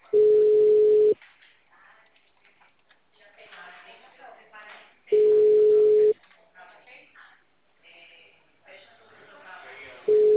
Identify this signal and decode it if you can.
Ringback tone (Europe, 425 Hz), with speech.
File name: out- -2002- - - .31941